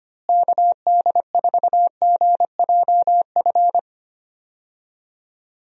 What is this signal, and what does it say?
Morse code, heard as a keyed tone, about 25 wpm, 700 Hz. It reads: KD4GJF
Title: USA Call Signs - 6 Characters - Repeated 3x - 25wpm